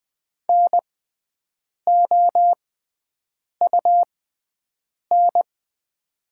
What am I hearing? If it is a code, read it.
Morse code, heard as a keyed tone, about 20 wpm, spaced as for 7 wpm, 700 Hz. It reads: NOUN
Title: 6x Character Spacing - Top 500 Words - 20wpm